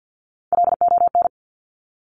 Morse code, heard as a keyed tone, about 50 wpm, 700 Hz. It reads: LON
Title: Top Three Letter Combinations in Words 50wpm